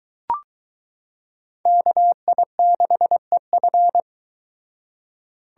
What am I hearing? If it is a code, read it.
Morse code, heard as a keyed tone, about 23 wpm, 700 Hz. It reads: KI6EF